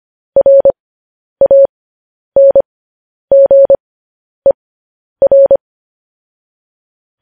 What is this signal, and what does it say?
Morse code, heard as a keyed tone, about 25 wpm, spaced as for 10 wpm, 550 Hz. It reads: RANGER
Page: news-headlines-in-morse-code-at-10-wpm